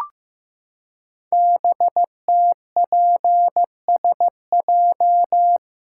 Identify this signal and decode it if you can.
Morse code, heard as a keyed tone, about 15 wpm, 700 Hz. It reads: BTPSJ
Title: Code Groups - 5 Letters - 15wpm